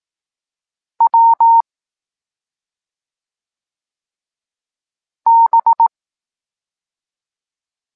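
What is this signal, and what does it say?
Morse code, heard as a keyed tone, about 18 wpm, 920 Hz. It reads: W B
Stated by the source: Evenly spaced Morse lesson text 21 for learning letters G and Z, from SP9MOA Niepołomicki Klub Krótkofalowców.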